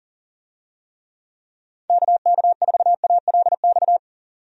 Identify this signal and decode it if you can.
Morse code, heard as a keyed tone, about 37 wpm, 700 Hz. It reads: KK4ALX
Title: USA Call Signs - 6 Characters - 40wpm